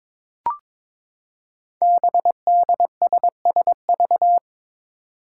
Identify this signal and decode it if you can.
Morse code, heard as a keyed tone, about 22 wpm, 700 Hz. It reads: BDSSV
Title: Five Challenging Letter-Number Combinations 22wpm